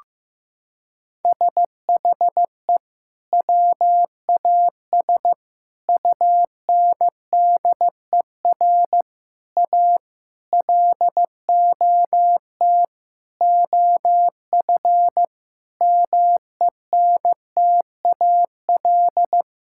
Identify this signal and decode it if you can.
Morse code, heard as a keyed tone, about 15 wpm, 700 Hz. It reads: SHE WAS UNDER A LOT OF MENTAL
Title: Sentences from Top 1000 Words - Review Part 6 - 15wpm